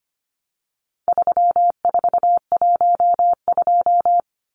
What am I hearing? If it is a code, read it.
Morse code, heard as a keyed tone, about 25 wpm, 700 Hz. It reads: 3412
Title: Numbers - 4-Digit 25wpm